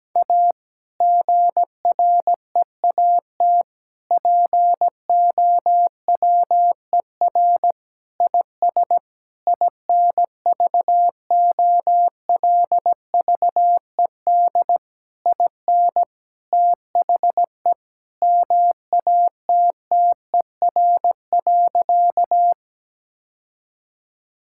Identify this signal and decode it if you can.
Morse code, heard as a keyed tone, about 17 wpm, 700 Hz. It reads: A GREAT POWER IS INVOLVED IN THE MATTER.
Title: Sentences from Top 1000 Words - Review Part 5 - 17wpm